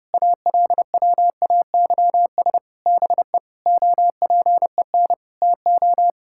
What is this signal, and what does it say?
Morse code, heard as a keyed tone, about 32 wpm, 700 Hz. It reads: ALWAYS BE OPEN TO
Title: Sentences from Top 1600 Words 30wpm